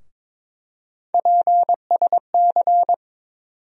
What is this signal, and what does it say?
Morse code, heard as a keyed tone, about 21 wpm, 700 Hz. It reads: PSC
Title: USA Call Sign Postfixes - 3 Letters 22wpm